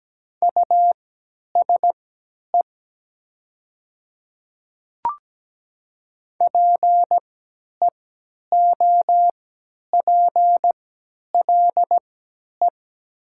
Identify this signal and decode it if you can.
Morse code, heard as a keyed tone, about 17 wpm, spaced as for 10 wpm, 700 Hz. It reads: USE PEOPLE
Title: 3x Character Spacing - Top 100 Words - 17wpm